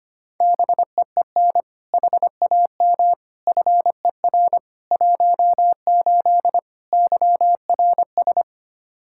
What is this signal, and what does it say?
Morse code, heard as a keyed tone, about 25 wpm, 700 Hz. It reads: BEEN HAM FER 18 YRS
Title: Practice QSO - Exchange 3 - More Info - 25wpm